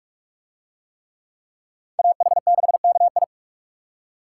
Morse code, heard as a keyed tone, about 44 wpm, 700 Hz. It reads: AH6KI